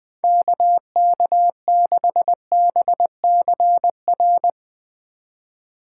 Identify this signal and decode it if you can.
Morse code, heard as a keyed tone, about 21 wpm, 700 Hz. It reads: KK6BCR
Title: Speed Racing - USA 2x3 Call Signs 20wpm